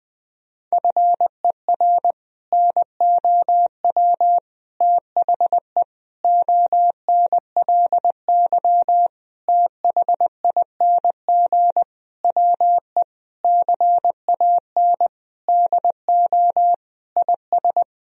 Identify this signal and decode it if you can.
Morse code, heard as a keyed tone, about 20 wpm, 700 Hz. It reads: FER NOW THE ONLY THING WE CAN DO IS
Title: Sentences from Top 200 Words and Abbreviations 20wpm